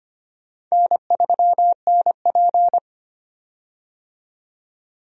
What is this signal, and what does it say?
Morse code, heard as a keyed tone, about 25 wpm, 700 Hz. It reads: N3NP
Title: USA Call Signs - 4 Characters - 25wpm